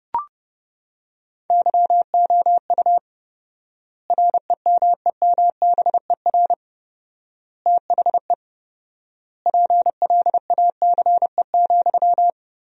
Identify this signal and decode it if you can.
Morse code, heard as a keyed tone, about 32 wpm, 700 Hz. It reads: YOU REMEMBER THE PLACE,
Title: 4x Word-Spacing - Sentences from Top 400 Words - 30wpm